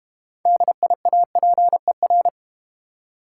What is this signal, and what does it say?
Morse code, heard as a keyed tone, about 32 wpm, 700 Hz. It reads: DIAPER